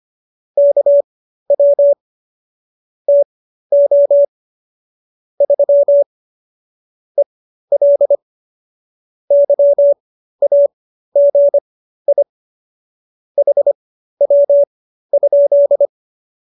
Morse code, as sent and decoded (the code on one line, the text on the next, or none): -.- .-- / - --- / ...-- / . .-.. / -.-- .- --. .. / .... .-- ..--..
KW TO 3 EL YAGI HW?